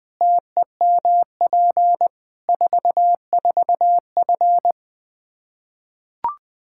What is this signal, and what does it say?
Morse code, heard as a keyed tone, about 21 wpm, 700 Hz. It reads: TEMP 44F
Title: Practice QSO - Exchange 2 - Info - 20wpm